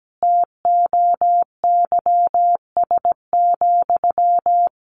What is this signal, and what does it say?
Morse code, heard as a keyed tone, about 17 wpm, 700 Hz. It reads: TOYS,
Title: A Sound Of Thunder 17wpm